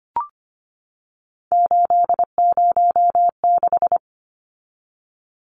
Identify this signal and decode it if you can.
Morse code, heard as a keyed tone, about 25 wpm, 700 Hz. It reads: 806
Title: Numbers - 3-Digit 25wpm